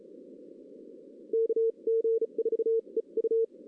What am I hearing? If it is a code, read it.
Morse code, heard as a keyed tone, about 34 wpm, spaced as for 28 wpm, 470 Hz. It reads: KG4EU
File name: 2016 arrl dx cw / 2016-ARRL-DX-CW-K5TR- - 0-3